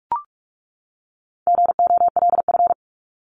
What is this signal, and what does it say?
Morse code, heard as a keyed tone, about 44 wpm, 700 Hz. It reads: GOLF
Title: Top 1751-2000 Words 45wpm